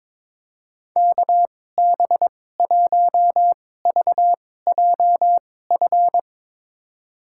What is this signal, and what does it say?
Morse code, heard as a keyed tone, about 22 wpm, spaced as for 16 wpm, 700 Hz. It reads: KB1VJF
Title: USA Call Signs - 6 Characters - 2x Character Spacing - 22wpm